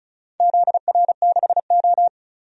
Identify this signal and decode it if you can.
Morse code, heard as a keyed tone, about 34 wpm, 700 Hz. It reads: ZR6O
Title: DXCC-100 Call Signs - 4 Characters - 35wpm